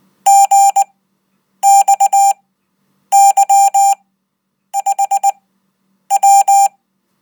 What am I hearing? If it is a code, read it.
Morse code, heard as a keyed tone, about 19 wpm, spaced as for 8 wpm, 790 Hz. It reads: GXY5W